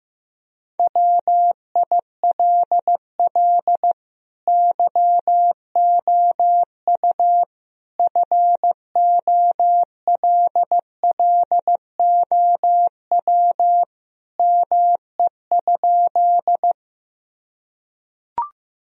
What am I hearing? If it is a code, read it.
Morse code, heard as a keyed tone, about 15 wpm, 700 Hz. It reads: WILL YOU FOLLOW ME?
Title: Sentences from Top 1000 Words - Review Part 1 - 15wpm